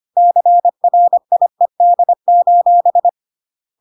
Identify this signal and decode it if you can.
Morse code, heard as a keyed tone, about 25 wpm, 700 Hz. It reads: CRIED: